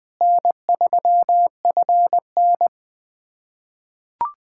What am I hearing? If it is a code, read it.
Morse code, heard as a keyed tone, about 20 wpm, 700 Hz. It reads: N3FN
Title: USA Callsign DE Callsign - 4 Characters - 20wpm